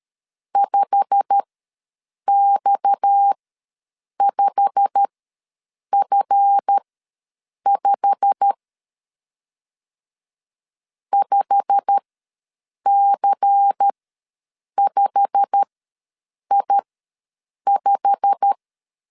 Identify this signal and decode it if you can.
Morse code, heard as a keyed tone, about 13 wpm, spaced as for 7 wpm, 800 Hz. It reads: 5X5F5 5C5I5
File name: LEZIONI / 8 YJ45 / 8 YJ45 ws 12wpm 3146